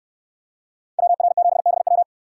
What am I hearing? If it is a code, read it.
Morse code, heard as a keyed tone, about 68 wpm, 700 Hz. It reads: WR7BW